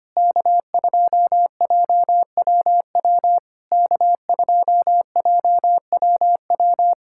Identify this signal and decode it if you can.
Morse code, heard as a keyed tone, about 25 wpm, 690 Hz. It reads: K2JWW K2JWW
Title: SOTA QSO Practice 25wpm